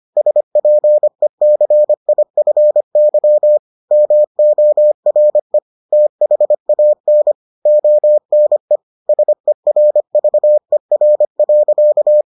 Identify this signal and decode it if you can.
Morse code, heard as a keyed tone, about 25 wpm, 600 Hz. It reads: SPECIFY MORE THAN ONE SERVER.